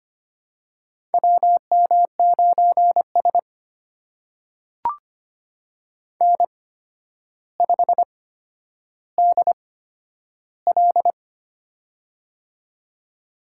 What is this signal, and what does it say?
Morse code, heard as a keyed tone, about 25 wpm, spaced as for 11 wpm, 700 Hz. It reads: WM9S N5DL
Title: USA Call Signs - 4 Characters - 8x Character Spacing - 25wpm